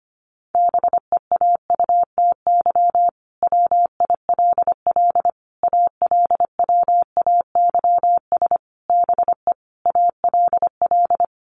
Morse code, heard as a keyed tone, about 25 wpm, 700 Hz. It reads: BEAUTY WILL ALWAYS BE ALL
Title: Sentences from Top 1300 Words 25wpm